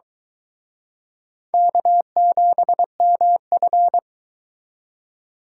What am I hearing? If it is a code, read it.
Morse code, heard as a keyed tone, about 23 wpm, 700 Hz. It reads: K7MF